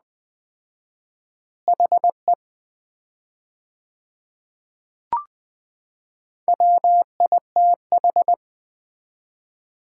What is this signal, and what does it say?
Morse code, heard as a keyed tone, about 20 wpm, 700 Hz. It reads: HE WITH
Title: 4x Word-Spacing - Sets of 2 Words - 20wpm